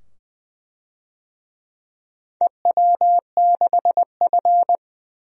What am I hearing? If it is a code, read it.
Morse code, heard as a keyed tone, about 20 wpm, 700 Hz. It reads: EW6F
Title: DXCC-100 Call Signs - 4 Characters - 20wpm